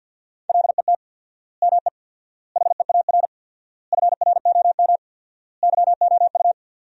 Morse code, heard as a keyed tone, about 50 wpm, 700 Hz. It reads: LET ME HEAR FROM YOU
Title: 4x Word-Spacing - Sentences from Top 400 Words - 50wpm